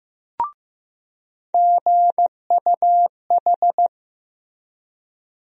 Morse code, heard as a keyed tone, about 15 wpm, 700 Hz. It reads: GUH
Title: USA Call Sign Postfixes - 3 Letters 15wpm